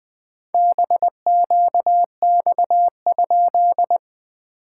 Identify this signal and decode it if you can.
Morse code, heard as a keyed tone, about 20 wpm, 700 Hz. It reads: BQX?
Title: SOTA QSO Practice 20wpm